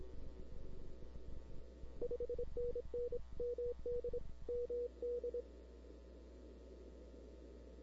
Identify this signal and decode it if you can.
Morse code, heard as a keyed tone, about 26 wpm, 480 Hz. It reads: 5NN MD MD
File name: CW / 2005-k5tr arrl cw dx -00:00